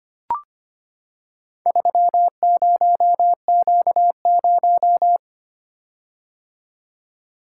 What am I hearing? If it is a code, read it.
Morse code, heard as a keyed tone, about 25 wpm, 700 Hz. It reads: 30Q0
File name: practice / 25wpm / Four Letter-Number Combinations - 25wpm